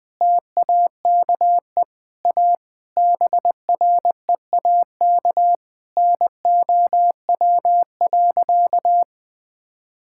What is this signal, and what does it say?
Morse code, heard as a keyed tone, about 20 wpm, 700 Hz. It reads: TAKE A BREAK NOW.
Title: Sentences with 5-Word Phrases - 20wpm